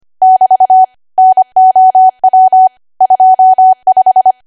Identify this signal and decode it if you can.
Morse code, heard as a keyed tone, about 25 wpm, 750 Hz. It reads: <BT> NOW 25
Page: 25 WPM Code Archive